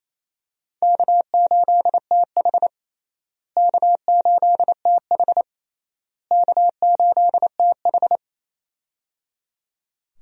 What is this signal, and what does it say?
Morse code, heard as a keyed tone, about 28 wpm, 700 Hz. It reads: K8TH K8TH K8TH
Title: USA Call Signs - 4 Characters - Repeated 3x - 28wpm